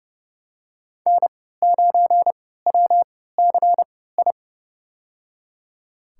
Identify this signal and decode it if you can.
Morse code, heard as a keyed tone, about 30 wpm, spaced as for 17 wpm, 700 Hz. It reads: N9WCI
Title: USA Call Signs - 5 Characters - 3x Character Spacing - 30wpm